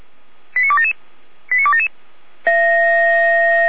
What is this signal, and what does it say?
Five-tone selective call (ZVEI), heard as quick runs of five tones, tones of about 70 ms. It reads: 89277 89277